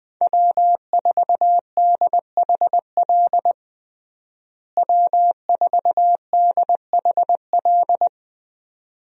Morse code, heard as a keyed tone, about 20 wpm, 700 Hz. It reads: W4DHL W4DHL
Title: USA Call Signs - 5 Characters - Repeated 2x - 20wpm